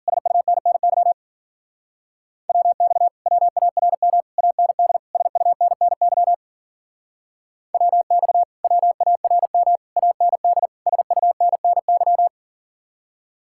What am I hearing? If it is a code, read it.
Morse code, heard as a keyed tone, about 50 wpm, 700 Hz. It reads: SUNNY WX WARM AND SUNNY WX WARM AND SUNNY
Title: Speed-Racing Common QSO Phrases - 40wpm